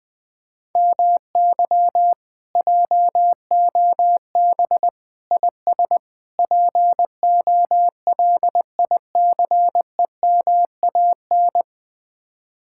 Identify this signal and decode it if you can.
Morse code, heard as a keyed tone, about 20 wpm, 700 Hz. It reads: MY JOB IS POLICEMAN
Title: Speed-Racing Common QSO Phrases - 17wpm